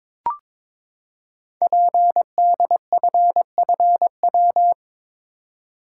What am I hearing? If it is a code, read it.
Morse code, heard as a keyed tone, about 22 wpm, 700 Hz. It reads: PDFFW
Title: Code Groups - 5 Letter:Numbers - 22wpm